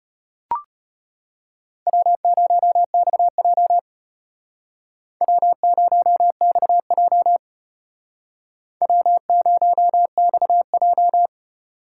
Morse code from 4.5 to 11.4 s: W0XJ W0XJ

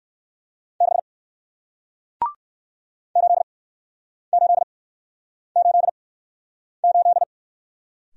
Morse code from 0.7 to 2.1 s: 6

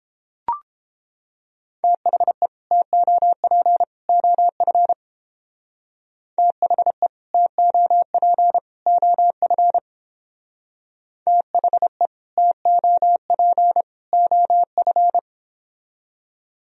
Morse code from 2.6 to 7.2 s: TOP OF THE

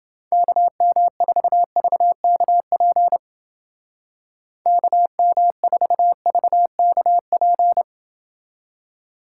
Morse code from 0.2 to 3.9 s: KM4VKP

